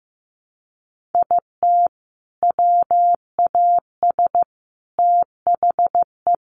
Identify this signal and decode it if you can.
Morse code, heard as a keyed tone, about 15 wpm, 700 Hz. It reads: IT WAS THE
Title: Sets of 5 Words 15wpm